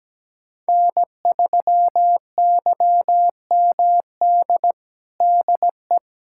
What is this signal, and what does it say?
Morse code, heard as a keyed tone, about 17 wpm, 700 Hz. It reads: N3YMD DE
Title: USA Callsign DE Callsign - 5 Characters - 17wpm